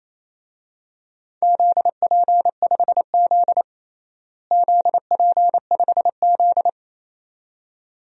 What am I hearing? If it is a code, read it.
Morse code, heard as a keyed tone, about 28 wpm, 700 Hz. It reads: ZP5Z ZP5Z
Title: DXCC-100 Call Signs - 4 Characters - Repeated 2x - 28wpm